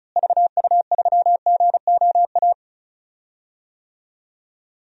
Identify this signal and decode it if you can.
Morse code, heard as a keyed tone, about 36 wpm, 700 Hz. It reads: VU3GOA
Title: DXCC-100 Call Signs - 6 Characters - Repeated 2x - 35wpm